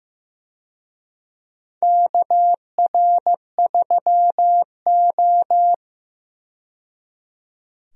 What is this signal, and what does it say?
Morse code, heard as a keyed tone, about 15 wpm, 700 Hz. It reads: KR3O
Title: USA Call Signs - 4 Characters - 15wpm